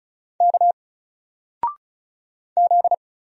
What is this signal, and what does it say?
Morse code, heard as a keyed tone, about 36 wpm, 700 Hz. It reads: K Z
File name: Single-Letter-Number-35wpm